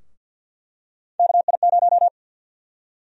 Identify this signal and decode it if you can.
Morse code, heard as a keyed tone, about 49 wpm, 700 Hz. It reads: KI0
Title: USA Call Sign Prefixes - Repeated 2x - 50wpm